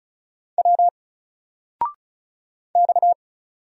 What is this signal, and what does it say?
Morse code, heard as a keyed tone, about 36 wpm, 700 Hz. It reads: W X
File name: Single-Letter-Number-35wpm